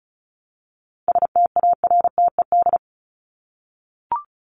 Morse code, heard as a keyed tone, about 34 wpm, 700 Hz. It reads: STARTED
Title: ED Suffix - Top 100 Words with Base Words 35wpm